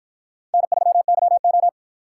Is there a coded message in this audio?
Yes: Morse code from 0.5 to 1.7 s, heard as a keyed tone, about 53 wpm, 700 Hz. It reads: N3YO